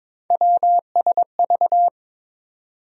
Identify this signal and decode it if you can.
Morse code, heard as a keyed tone, about 22 wpm, 700 Hz. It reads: WSV
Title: USA Call Sign Postfixes - 3 Letters 22wpm